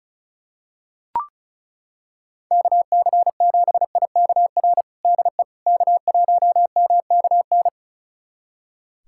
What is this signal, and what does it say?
Morse code, heard as a keyed tone, about 36 wpm, 700 Hz. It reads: KC7IKR DE K1MKN
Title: USA Callsign DE Callsign 35wpm